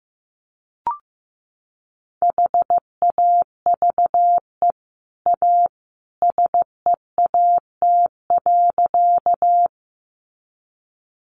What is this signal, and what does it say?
Morse code, heard as a keyed tone, about 15 wpm, 700 Hz. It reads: HAVE A SEAT.
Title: Sentences from Top 1500 Words 15wpm